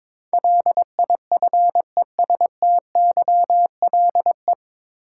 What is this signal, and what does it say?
Morse code, heard as a keyed tone, about 23 wpm, 700 Hz. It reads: LIFESTYLE
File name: 9 Letter Words - First 3 Letters Given - 22wpm